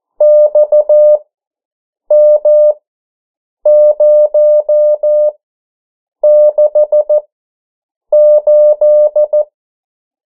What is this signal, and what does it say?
Morse code, heard as a keyed tone, about 14 wpm, spaced as for 7 wpm, 600 Hz. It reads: XM068
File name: morse mp3 / 09 urok 8